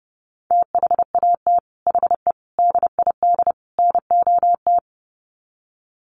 Morse code, heard as a keyed tone, about 30 wpm, 700 Hz. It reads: THAT HE DID NOT
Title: Sets of 4 Words 30wpm